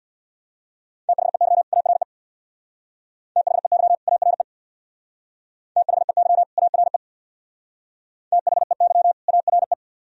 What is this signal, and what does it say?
Morse code, heard as a keyed tone, about 62 wpm, 700 Hz. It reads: THEY ARE THEY ARE THEY ARE THEY ARE
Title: Speed-Racing - Sets of 2 Words 50wpm